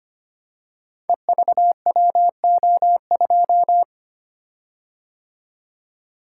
Morse code, heard as a keyed tone, about 25 wpm, 700 Hz. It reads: EVWO2
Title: Code Groups - 5 Letter:Numbers - 25wpm